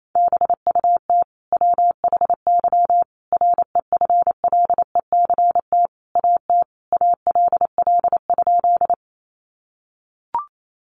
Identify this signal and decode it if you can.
Morse code, heard as a keyed tone, about 28 wpm, 700 Hz. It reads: BUT WHY REFLECT AT ALL?
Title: Sentences from Top 1500 Words 28wpm